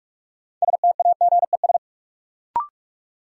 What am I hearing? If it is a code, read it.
Morse code, heard as a keyed tone, about 42 wpm, 700 Hz. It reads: STAGES
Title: Sets of 5 Words Encore - New Word Drill - 45wpm